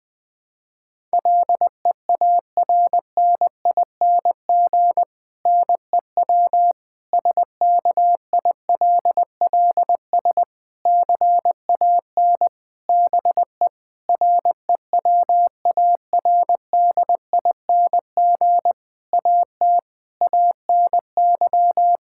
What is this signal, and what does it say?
Morse code, heard as a keyed tone, about 20 wpm, 700 Hz. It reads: LEARNING NEW SKILLS CAN BE REWARDING AT ANY
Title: Difficult Sentences with Hints - 20wpm